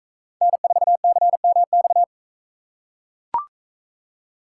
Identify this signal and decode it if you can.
Morse code, heard as a keyed tone, about 40 wpm, 700 Hz. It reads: N4CMX